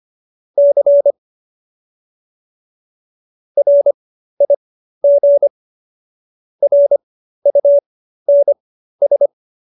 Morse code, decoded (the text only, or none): C RIG RUNS